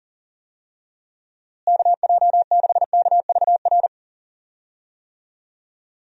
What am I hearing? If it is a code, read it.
Morse code, heard as a keyed tone, about 40 wpm, 700 Hz. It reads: KJ6KVR
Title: USA Call Signs - 6 Characters - Repeated 2x - 40wpm